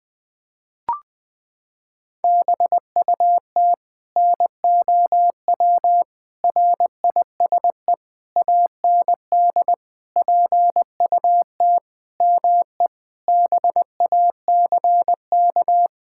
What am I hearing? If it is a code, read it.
Morse code, heard as a keyed tone, about 20 wpm, 700 Hz. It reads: BUT NOW RISE AND PUT ME BACK
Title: Sentences from Top 1000 Words - Review Part 6 - 20wpm